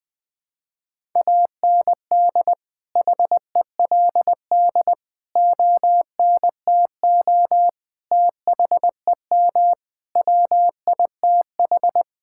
Morse code, heard as a keyed tone, about 20 wpm, 700 Hz. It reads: AND HELD ONTO THEM WITH
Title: Sets of 9 Words - 20wpm